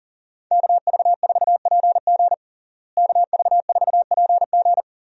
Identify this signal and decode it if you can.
Morse code, heard as a keyed tone, about 40 wpm, 700 Hz. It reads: KV4PG KV4PG
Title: USA Call Signs - 5 Characters - Repeated 3x - 40wpm